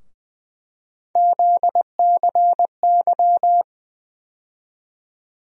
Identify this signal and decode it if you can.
Morse code, heard as a keyed tone, about 20 wpm, 700 Hz. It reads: ZCY
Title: USA Call Sign Postfixes - 3 Letters 20wpm